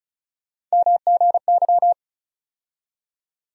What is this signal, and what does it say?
Morse code, heard as a keyed tone, about 34 wpm, 700 Hz. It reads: MGY
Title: USA Call Sign Postfixes - 3 Letters 35wpm